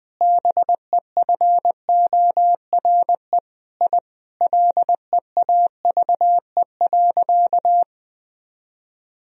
Morse code, heard as a keyed tone, about 20 wpm, 700 Hz. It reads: BEFORE I LEAVE.
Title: Sentences from Top 1000 Words - Review Part 1 - 20wpm